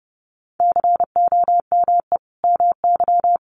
Morse code, heard as a keyed tone, about 30 wpm, 700 Hz. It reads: COME MY
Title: Sentences from Top 1300 Words 30wpm